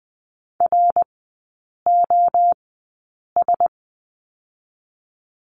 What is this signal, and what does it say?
Morse code, heard as a keyed tone, about 20 wpm, spaced as for 8 wpm, 700 Hz. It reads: ROS
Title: 053 Lesson 10e - Letter r - Three Characters 20wpm